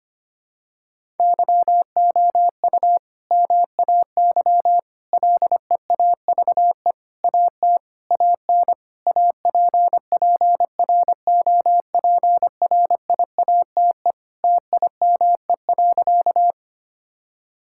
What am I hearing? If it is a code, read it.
Morse code, heard as a keyed tone, about 25 wpm, 700 Hz. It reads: YOU MAY LEAVE AT AN APPROPRIATE TIME.
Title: Sentences from Top 1000 Words - Review Part 5 - 25wpm